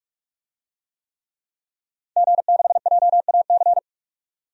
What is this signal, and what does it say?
Morse code, heard as a keyed tone, about 44 wpm, 700 Hz. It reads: G6JAC
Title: DXCC-100 Call Signs - 5 Characters - 45wpm